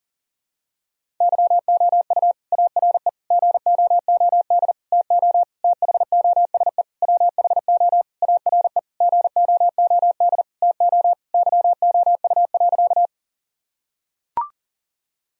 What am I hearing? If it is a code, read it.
Morse code, heard as a keyed tone, about 40 wpm, 700 Hz. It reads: YOU ARE GOOD TO THOSE WHO ARE GOOD TO YOU.
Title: Sentences from Top 1000 Words - Review Part 1 - 40wpm